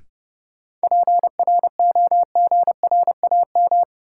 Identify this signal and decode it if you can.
Morse code, heard as a keyed tone, about 30 wpm, 700 Hz. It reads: PROGRAM
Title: Sentences from Top 1500 Words - New Word Drill Alt - 30wpm